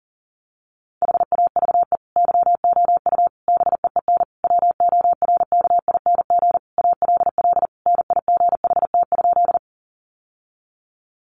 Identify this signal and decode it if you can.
Morse code, heard as a keyed tone, about 40 wpm, 700 Hz. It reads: HAVE YOU BEEN WORKING ALL NIGHT?
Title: Sentences from Top 700 Words 40wpm